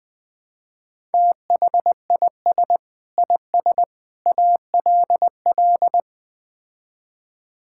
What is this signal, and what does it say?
Morse code, heard as a keyed tone, about 20 wpm, 700 Hz. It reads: THIS IS ALL
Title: Build A Phrase - 5-Word Phrases - 20wpm